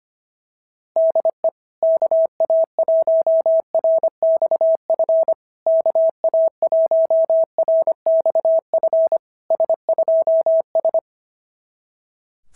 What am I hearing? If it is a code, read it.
Morse code, heard as a keyed tone, about 25 wpm, 650 Hz. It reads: DE KA1RXF KA1RXF S2S
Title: SOTA QSO Practice 25wpm